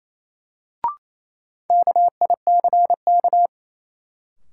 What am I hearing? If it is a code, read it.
Morse code, heard as a keyed tone, about 27 wpm, 700 Hz. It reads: KICK
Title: Sentences from Top 1300 Words - New Word Drill - 28wpm